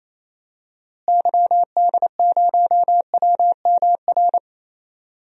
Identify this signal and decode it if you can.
Morse code, heard as a keyed tone, about 28 wpm, 700 Hz. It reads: YD0WMR
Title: DXCC-100 Call Signs - 6 Characters - Repeated 3x - 28wpm